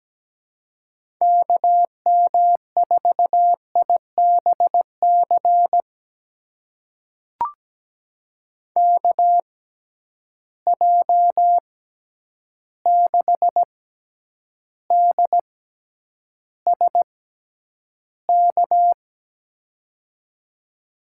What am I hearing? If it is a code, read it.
Morse code, heard as a keyed tone, about 17 wpm, spaced as for 9 wpm, 700 Hz. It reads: KM4IBC KJ6DSK
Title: USA Call Signs - 6 Characters - 6x Character Spacing - 17wpm